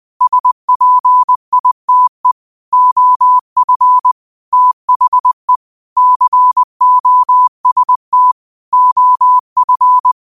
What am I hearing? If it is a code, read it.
Morse code, heard as a keyed tone, about 20 wpm, 1000 Hz. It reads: SPITE OF THE COST OF
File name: 20wpm